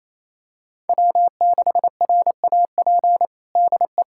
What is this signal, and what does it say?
Morse code, heard as a keyed tone, about 28 wpm, 700 Hz. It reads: W6RAP DE
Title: USA Callsign DE Callsign - 5 Characters - 28wpm